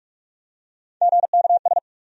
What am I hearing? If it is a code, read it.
Morse code, heard as a keyed tone, about 46 wpm, 700 Hz. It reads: GKS
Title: USA Call Sign Postfixes - 3 Letters 45wpm